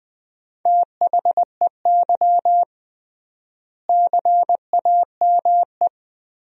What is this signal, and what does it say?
Morse code, heard as a keyed tone, about 20 wpm, 700 Hz. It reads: THEY CAME
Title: 3x Word-Spacing - Sentences from Top 300 Words - 20wpm